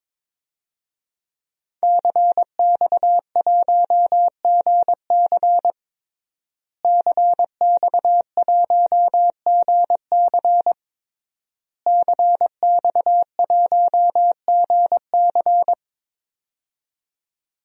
Morse code, heard as a keyed tone, about 22 wpm, 700 Hz. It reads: CX1GC CX1GC CX1GC
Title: DXCC-100 Call Signs - 5 Characters - Repeated 3x - 22wpm